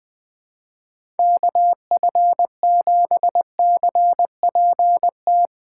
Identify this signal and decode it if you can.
Morse code, heard as a keyed tone, about 20 wpm, 700 Hz. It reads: KF7CPT
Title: Speed Racing - USA Call Signs 20wpm